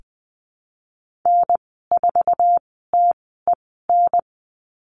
Morse code, heard as a keyed tone, about 20 wpm, spaced as for 14 wpm, 700 Hz. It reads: N4TEN
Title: 062 Lesson 11g - Letter h - Call Signs 20wpm